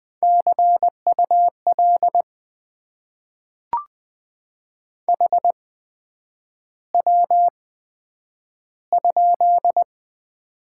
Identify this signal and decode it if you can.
Morse code, heard as a keyed tone, about 20 wpm, spaced as for 9 wpm, 700 Hz. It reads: CUL HW?
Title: Top 90 QSO Elements - 8x Character Spacing - 20wpm